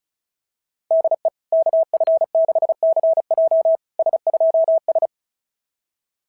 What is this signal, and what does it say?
Morse code, heard as a keyed tone, about 36 wpm, 650 Hz. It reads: DE KF6CJ S2S
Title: SOTA QSO Practice 35wpm